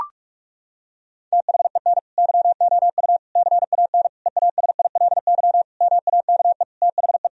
Morse code, heard as a keyed tone, about 42 wpm, 700 Hz. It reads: THEN YOU CAN EASILY MAKE THE
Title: Sentences from Top 1000 Words - Review Part 1 - 45wpm